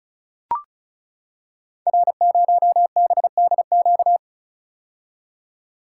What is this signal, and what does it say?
Morse code, heard as a keyed tone, about 36 wpm, 690 Hz. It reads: R0BDQ
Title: Code Groups - 5 Letter:Numbers - 35wpm